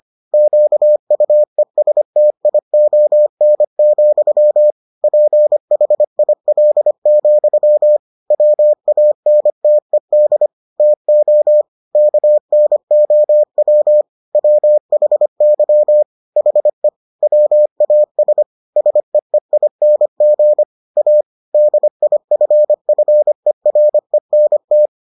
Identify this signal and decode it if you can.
Morse code, heard as a keyed tone, about 25 wpm, 600 Hz. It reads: QUESTION, PHIL, WANTED TO KNOW WHY HE WAS SEEING A DIFFERENT